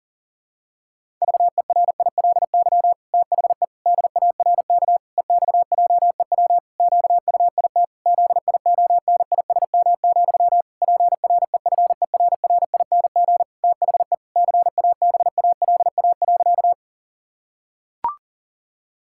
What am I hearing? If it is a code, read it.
Morse code, heard as a keyed tone, about 40 wpm, 700 Hz. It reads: VERILY THE DARK EXJEW QUIT ZIONISM, PREFERRING THE CABALA.
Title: Pangram Challenge v2 - 40wpm